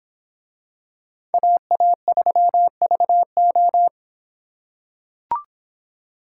Morse code, heard as a keyed tone, about 26 wpm, 700 Hz. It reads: AA3VO